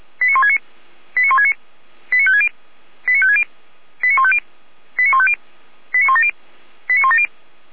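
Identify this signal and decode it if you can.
Five-tone selective call (ZVEI), heard as quick runs of five tones, tones of about 70 ms. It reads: 89169 89169 89560 89560 89150 89150 89170 89170
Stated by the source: ZVEI five-tone call 89220 received by SDR, fire brigade Feuerwehr Orlamünde.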